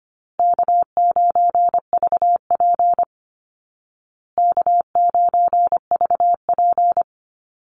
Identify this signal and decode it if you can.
Morse code, heard as a keyed tone, about 25 wpm, 700 Hz. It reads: K9VP K9VP
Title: Call Signs Repeated 2x - 25wpm